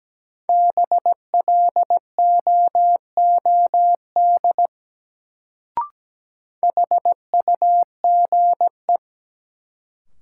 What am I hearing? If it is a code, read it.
Morse code, heard as a keyed tone, about 17 wpm, 700 Hz. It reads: BLOOD HUGE
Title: Sentences from Top 1100 Words - New Word Drill Alt - 17wpm